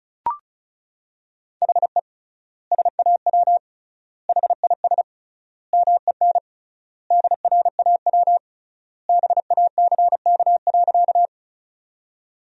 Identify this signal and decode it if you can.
Morse code, heard as a keyed tone, about 36 wpm, 700 Hz. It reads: HE SAW HIS MEN DRAW BACK.
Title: 3x Word-Spacing - Sentences from Top 300 Words - 35wpm